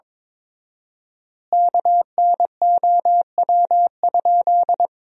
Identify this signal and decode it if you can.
Morse code, heard as a keyed tone, about 23 wpm, 700 Hz. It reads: KNOW?
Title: 4x Word-Spacing - Sentences from Top 100 Words - 22wpm